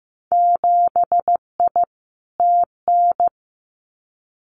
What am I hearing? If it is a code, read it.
Morse code, heard as a keyed tone, about 15 wpm, 700 Hz. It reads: 7I TN
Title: Winter Field Day Exchanges Encore 15wpm